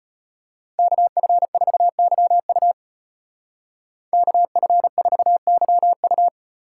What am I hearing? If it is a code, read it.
Morse code, heard as a keyed tone, about 32 wpm, 700 Hz. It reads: KF4YU KF4YU